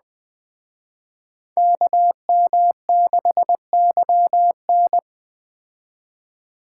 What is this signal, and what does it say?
Morse code, heard as a keyed tone, about 20 wpm, 700 Hz. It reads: KM6YN
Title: Speed Racing - USA 2x2 Call Signs 17wpm